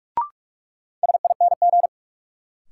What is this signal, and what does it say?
Morse code, heard as a keyed tone, about 46 wpm, 700 Hz. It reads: SING
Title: Sentences from Top 400 Words - New Word Drill - 45wpm